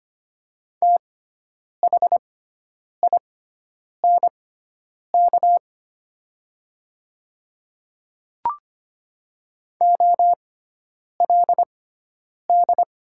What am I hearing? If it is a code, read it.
Morse code, heard as a keyed tone, about 25 wpm, spaced as for 9 wpm, 700 Hz. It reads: THINK OLD
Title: 6x Character Spacing - Top 200 Words - 25wpm